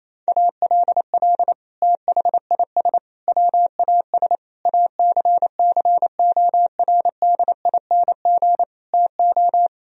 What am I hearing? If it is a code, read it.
Morse code, heard as a keyed tone, about 28 wpm, 700 Hz. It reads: ALL THIS WAS ACCORDING TO
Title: Sentences from Top 1000 Words - Review Part 4 - 28wpm